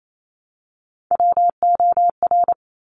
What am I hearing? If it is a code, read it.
Morse code, heard as a keyed tone, about 28 wpm, 700 Hz. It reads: WOR
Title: Top Three Letter Combinations in Words 28wpm